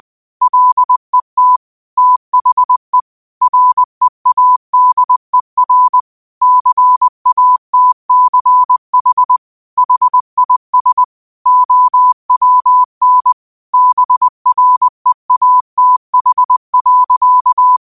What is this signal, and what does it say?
Morse code, heard as a keyed tone, about 20 wpm, 1000 Hz. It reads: LET THE READER CATCH HIS OWN BREATH.
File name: qotdfiles / 20wpm